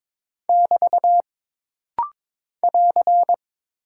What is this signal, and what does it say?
Morse code, heard as a keyed tone, about 23 wpm, 700 Hz. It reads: <BT> <AR>
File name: practice / 22wpm / Single Punctuation-Prosign - Rapid-Fire - 22wpm